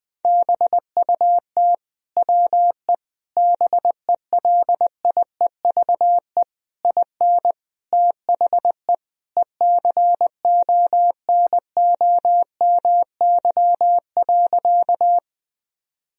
Morse code, heard as a keyed tone, about 20 wpm, 700 Hz. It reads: BUT WE BELIEVE IN THE ECONOMY.
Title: Sentences from Top 1000 Words - Review Part 3 - 20wpm